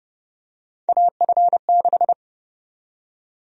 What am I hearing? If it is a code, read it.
Morse code, heard as a keyed tone, about 32 wpm, 700 Hz. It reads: AF6
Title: USA Call Sign Prefixes - 30wpm